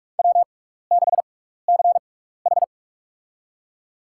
Morse code, heard as a keyed tone, about 42 wpm, spaced as for 15 wpm, 700 Hz. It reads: W6CH
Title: USA Call Signs - 4 Characters - 6x Character Spacing - 45wpm